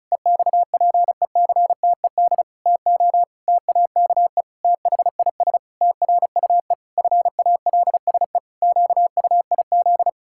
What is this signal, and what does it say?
Morse code, heard as a keyed tone, about 36 wpm, 700 Hz. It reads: EXPECTED TO TAKE THIS TRUE FALSE QUIZ
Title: Pangram Challenge v2 - 35wpm